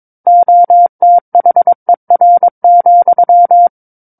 Morse code, heard as a keyed tone, about 22 wpm, 700 Hz. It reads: OTHER,
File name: 06-Chapter-06 22 WPM